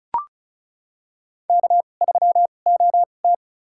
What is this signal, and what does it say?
Morse code, heard as a keyed tone, about 34 wpm, spaced as for 25 wpm, 700 Hz. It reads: K3OT
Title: USA Call Signs - 4 Characters - 2x Character Spacing - 35wpm